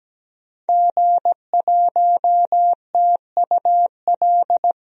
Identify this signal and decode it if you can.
Morse code, heard as a keyed tone, about 17 wpm, 700 Hz. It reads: G1TUL
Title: DXCC-100 Call Signs - 5 Characters - Repeated 3x - 17wpm